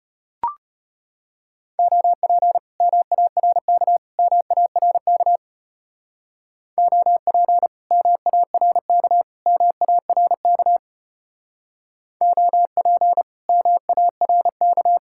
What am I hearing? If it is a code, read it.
Morse code, heard as a keyed tone, about 33 wpm, 700 Hz. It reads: OP MARK MARK OP MARK MARK OP MARK